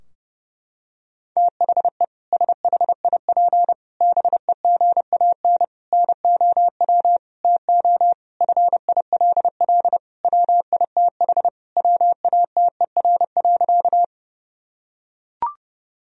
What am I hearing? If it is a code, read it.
Morse code, heard as a keyed tone, about 30 wpm, 700 Hz. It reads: THE SHIP BEGAN NOW TO FILL WITH WATER.
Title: Sentences from Top 1000 Words - Review Part 3 - 30wpm